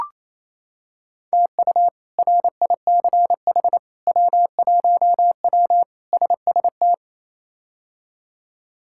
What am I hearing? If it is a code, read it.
Morse code, heard as a keyed tone, about 27 wpm, 700 Hz. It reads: TU RICH W1W SST
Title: K1USN SST QSO Practice - 28wpm